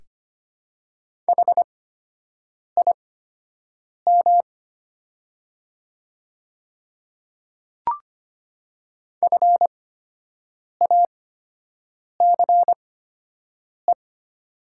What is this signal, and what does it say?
Morse code, heard as a keyed tone, about 25 wpm, spaced as for 7 wpm, 690 Hz. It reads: HIM FACE
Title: 8x Character Spacing - Top 300 Words - 25wpm